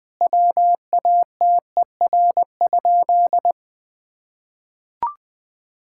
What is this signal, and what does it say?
Morse code, heard as a keyed tone, about 20 wpm, 700 Hz. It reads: WATER?
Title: Sentences from Top 1000 Words - Review Part 5 - 20wpm